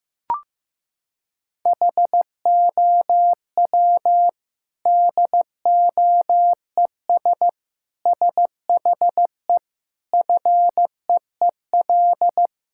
Morse code, heard as a keyed tone, about 15 wpm, 700 Hz. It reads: HOW DOES SHE FEEL
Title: Sentences from Top 1000 Words - Review Part 6 - 15wpm